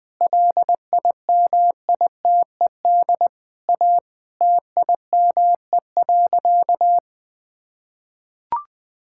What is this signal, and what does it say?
Morse code, heard as a keyed tone, about 20 wpm, 700 Hz. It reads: LIMITED A TIME.
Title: Sentences from Top 1000 Words - Review Part 6 - 20wpm